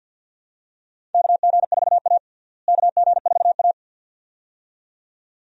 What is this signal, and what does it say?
Morse code, heard as a keyed tone, about 51 wpm, 700 Hz. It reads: KG4A KG4A
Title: USA Call Signs - 4 Characters - Repeated 2x - 50wpm